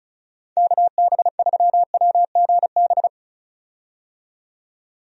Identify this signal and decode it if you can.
Morse code, heard as a keyed tone, about 34 wpm, 700 Hz. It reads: KB3WGB
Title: USA Call Signs - 6 Characters - 35wpm